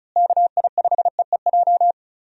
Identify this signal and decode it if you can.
Morse code, heard as a keyed tone, about 36 wpm, 700 Hz. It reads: KI5EEJ
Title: USA Call Signs - 6 Characters - Repeated 3x - 35wpm